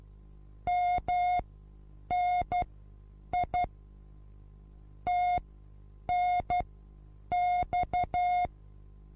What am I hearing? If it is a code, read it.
Morse code, heard as a keyed tone, about 12 wpm, spaced as for 8 wpm, 720 Hz. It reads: MNI TNX
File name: repos / Morsekurs / DJ4UF Morsen23